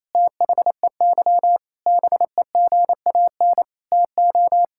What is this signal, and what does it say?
Morse code, heard as a keyed tone, about 27 wpm, 700 Hz. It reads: THEY BEGAN TO
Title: Sentences from Top 1000 Words - Review Part 6 - 28wpm